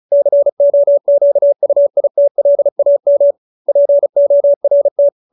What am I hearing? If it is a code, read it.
Morse code, heard as a keyed tone, about 34 wpm, 570 Hz. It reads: COQUITLAM PORT
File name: QSO314 35